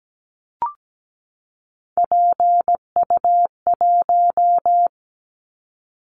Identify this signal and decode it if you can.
Morse code, heard as a keyed tone, about 17 wpm, 700 Hz. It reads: PU1
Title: Callsign Prefixes for Top 50 DXCC - 17wpm